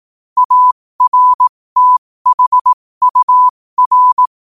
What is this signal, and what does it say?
Morse code, heard as a keyed tone, about 18 wpm, spaced as for 15 wpm, 1000 Hz. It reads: ARTHUR
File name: qotdfiles / 15wpm